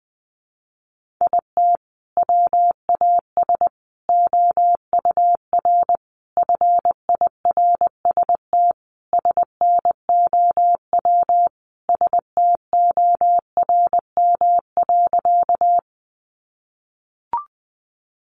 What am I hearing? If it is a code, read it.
Morse code, heard as a keyed tone, about 20 wpm, 700 Hz. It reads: IT WAS OUR FIRST SNOW STORM.
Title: Sentences from Top 1500 Words 20wpm